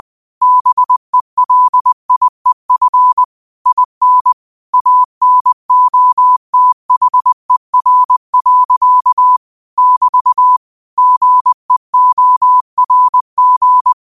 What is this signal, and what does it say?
Morse code, heard as a keyed tone, about 20 wpm, 1000 Hz. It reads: BELIEF IN ANOTHER. <BT> GEORG
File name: qotdfiles / 20wpm